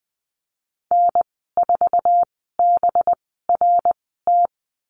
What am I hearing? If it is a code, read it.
Morse code, heard as a keyed tone, about 19 wpm, spaced as for 14 wpm, 700 Hz. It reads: N4BRT
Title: 207 Lesson 33g - Letter j - Call Signs 20wpm